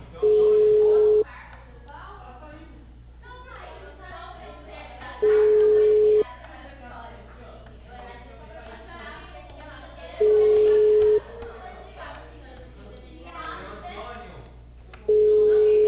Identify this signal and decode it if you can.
Ringback tone (Europe, 425 Hz), with speech.
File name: out- -2012- - - .75333